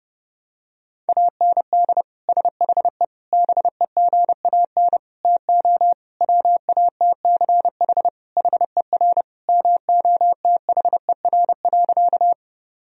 Morse code, heard as a keyed tone, about 30 wpm, 700 Hz. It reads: AND SHE BEGAN TO WATCH HER MOTHER.
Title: Sentences from Top 1000 Words - Review Part 1 - 30wpm